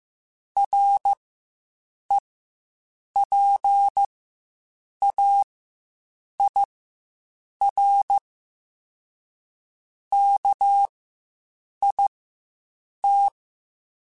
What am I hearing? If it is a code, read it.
Morse code, heard as a keyed tone, about 15 wpm, spaced as for 7 wpm, 800 Hz. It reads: REPAIR KIT
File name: Quotes Spaced / Quotes Spaced-mp3s / Airtight sp @10